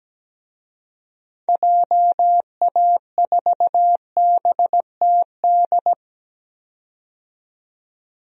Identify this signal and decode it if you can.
Morse code, heard as a keyed tone, about 17 wpm, 700 Hz. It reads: JA4BTD
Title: DXCC-100 Call Signs - 6 Characters - 17wpm